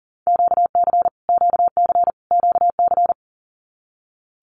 Morse code, heard as a keyed tone, about 40 wpm, 700 Hz. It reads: QC QC QC
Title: Canadian Provinces and Territories - Repeated 3x - 40wpm